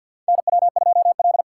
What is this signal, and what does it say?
Morse code, heard as a keyed tone, about 49 wpm, 700 Hz. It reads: NW2L
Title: USA Call Signs - 6x Character Spacing - 50wpm